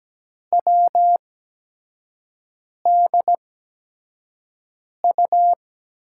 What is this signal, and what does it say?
Morse code, heard as a keyed tone, about 17 wpm, spaced as for 5 wpm, 700 Hz. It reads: WDU
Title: USA Call Signs - 6 Characters - 8x Character Spacing - 17wpm